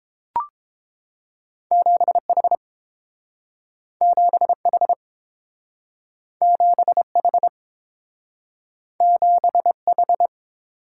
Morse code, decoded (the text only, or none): 7H 7H 7H 7H